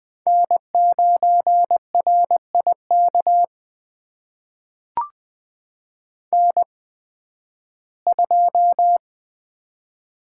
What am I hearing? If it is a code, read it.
Morse code, heard as a keyed tone, about 20 wpm, spaced as for 13 wpm, 700 Hz. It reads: N9RIK N2
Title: USA Call Signs - 5 Characters - 8x Character Spacing - 20wpm